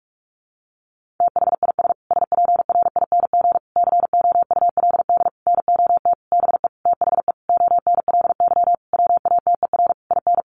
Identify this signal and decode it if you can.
Morse code, heard as a keyed tone, about 44 wpm, 700 Hz. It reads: THIS SPRING COULD NOT BE THE ONLY WATER IN
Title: Sentences from Top 1500 Words 45wpm